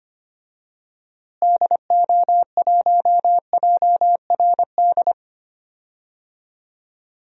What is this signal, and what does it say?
Morse code, heard as a keyed tone, about 25 wpm, 700 Hz. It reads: DO1JRD
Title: DXCC-100 Call Signs - 6 Characters - Repeated 3x - 25wpm